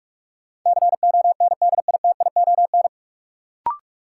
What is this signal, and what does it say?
Morse code, heard as a keyed tone, about 46 wpm, 700 Hz. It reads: CONDITION
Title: Sentences from Top 1000 Words - New Word Drill - 45wpm